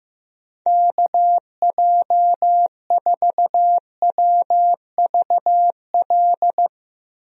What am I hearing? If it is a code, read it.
Morse code, heard as a keyed tone, about 15 wpm, 700 Hz. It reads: KJ4WVL
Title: USA Call Signs - 6 Characters - 15wpm